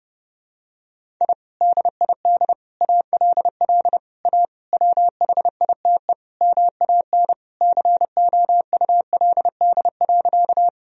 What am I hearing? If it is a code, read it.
Morse code, heard as a keyed tone, about 30 wpm, 700 Hz. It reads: I DID ALL A WHITE MAN COULD.
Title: Sentences from Top 1000 Words - Review Part 1 - 30wpm